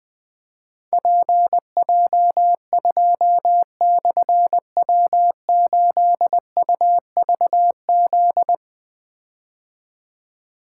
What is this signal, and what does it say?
Morse code, heard as a keyed tone, about 20 wpm, 700 Hz. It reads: PJ2/W8UVZ
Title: Difficult Call Signs 20wpm